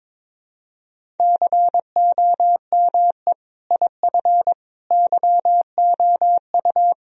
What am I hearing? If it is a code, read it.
Morse code, heard as a keyed tone, about 22 wpm, 700 Hz. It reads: COME IF YOU
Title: Sentences from Top 1000 Words - Review Part 1 - 22wpm